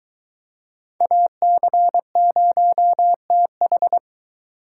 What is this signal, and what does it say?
Morse code, heard as a keyed tone, about 21 wpm, 700 Hz. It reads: AC0TH